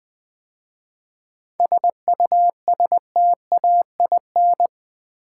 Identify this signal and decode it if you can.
Morse code, heard as a keyed tone, about 20 wpm, 700 Hz. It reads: SUSTAIN
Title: 7 Letter Words - First 3 Letters Given - 20wpm